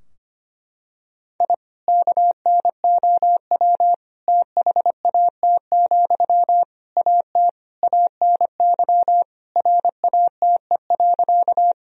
Morse code, heard as a keyed tone, about 25 wpm, 700 Hz. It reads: I KNOW THAT, AT ANY RATE.
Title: Sentences from Top 1000 Words - Review Part 4 - 25wpm